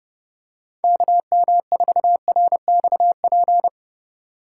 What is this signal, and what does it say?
Morse code, heard as a keyed tone, about 26 wpm, 700 Hz. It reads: KM4RXP